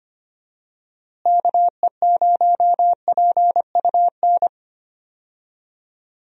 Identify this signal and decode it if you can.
Morse code, heard as a keyed tone, about 25 wpm, 700 Hz. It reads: KE0PUN
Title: USA Call Signs - 6 Characters - Repeated 2x - 25wpm